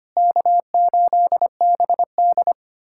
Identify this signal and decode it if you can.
Morse code, heard as a keyed tone, about 25 wpm, 700 Hz. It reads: K8BD
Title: USA Callsign DE Callsign - 4 Characters - 25wpm